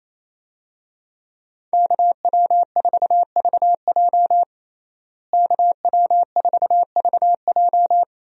Morse code, heard as a keyed tone, about 27 wpm, 700 Hz. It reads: KW4VJ KW4VJ
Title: USA Call Signs - 5 Characters - Repeated 3x - 28wpm